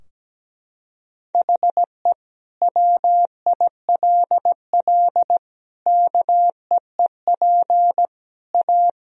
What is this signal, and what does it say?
Morse code, heard as a keyed tone, about 17 wpm, 700 Hz. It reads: HE WILL KEEP A
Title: Sentences from Top 1000 Words - Review Part 3 - 17wpm